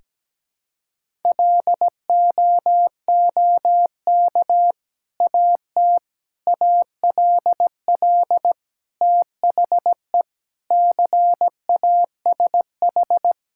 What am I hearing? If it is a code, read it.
Morse code, heard as a keyed tone, about 17 wpm, 700 Hz. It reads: LOOK AT ALL THE CASH
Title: Sentences from Top 1000 Words - Review Part 5 - 17wpm